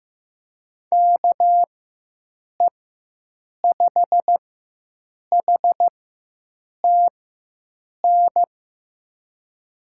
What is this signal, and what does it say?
Morse code, heard as a keyed tone, about 16 wpm, spaced as for 7 wpm, 700 Hz. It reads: KE5HTN
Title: USA Call Signs - 6 Characters - 4x Character Spacing - 15wpm